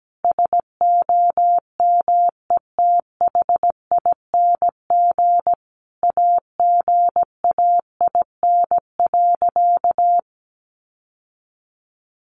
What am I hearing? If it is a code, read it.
Morse code, heard as a keyed tone, about 17 wpm, 700 Hz. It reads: SOMETHING AGAIN.
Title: Sentences from Top 1100 Words 17wpm